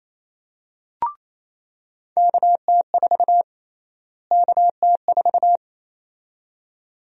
Morse code, heard as a keyed tone, about 28 wpm, 700 Hz. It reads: KT4 KT4
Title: USA Call Sign Prefixes - Repeated 2x - 28wpm